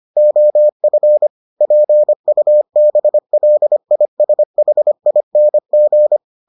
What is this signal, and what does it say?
Morse code, heard as a keyed tone, about 25 wpm, 600 Hz. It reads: OF PUBLISHING